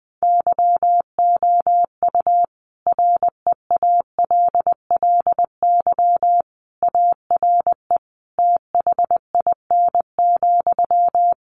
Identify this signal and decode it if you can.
Morse code, heard as a keyed tone, about 20 wpm, 700 Hz. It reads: YOU REALLY ARE THIN,
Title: Sentences from Top 1200 Words 20wpm